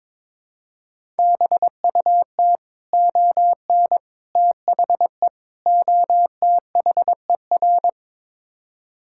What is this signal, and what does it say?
Morse code, heard as a keyed tone, about 22 wpm, 700 Hz. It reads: BUT ON THE OTHER
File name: practice / 22wpm / Build A Phrase - 5-Word Phrases - 22wpm